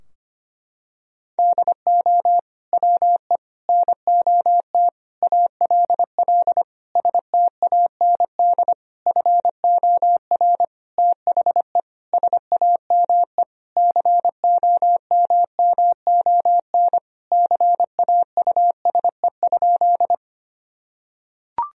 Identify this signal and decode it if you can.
Morse code, heard as a keyed tone, about 25 wpm, 700 Hz. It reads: DO WE NOT ALL STAND FOR THE SAME COMMON CAUSE?
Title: Sentences from Top 1000 Words - Review Part 3 - 25wpm